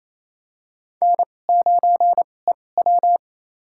Morse code, heard as a keyed tone, about 28 wpm, spaced as for 20 wpm, 700 Hz. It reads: N9EW
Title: USA Call Signs - 4 Characters - 2x Character Spacing - 28wpm